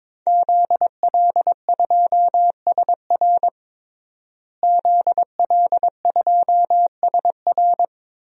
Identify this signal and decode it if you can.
Morse code, heard as a keyed tone, about 21 wpm, 700 Hz. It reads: ZL2SR ZL2SR
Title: DXCC-100 Call Signs - 5 Characters - Repeated 3x - 22wpm